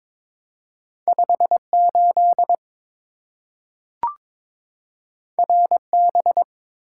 Morse code, heard as a keyed tone, about 22 wpm, 700 Hz. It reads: 58 RB
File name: practice / 22wpm / Two Letter-Number Combinations - 22wpm